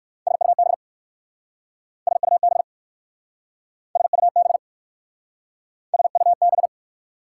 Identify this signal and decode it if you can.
Morse code, heard as a keyed tone, about 60 wpm, 690 Hz. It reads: SUB SUB SUB SUB